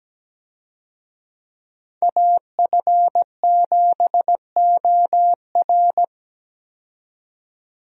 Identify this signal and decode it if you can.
Morse code, heard as a keyed tone, about 17 wpm, 700 Hz. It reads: AF7OR
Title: USA Call Signs - 5 Characters - 17wpm